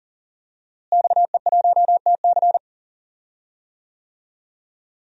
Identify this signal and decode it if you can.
Morse code, heard as a keyed tone, about 38 wpm, 700 Hz. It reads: XE1TC
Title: DXCC-100 Call Signs - 5 Characters - Repeated 3x - 40wpm